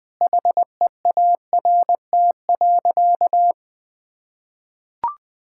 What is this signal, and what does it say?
Morse code, heard as a keyed tone, about 20 wpm, 700 Hz. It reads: HEART.
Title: Sentences from Top 1600 Words 20wpm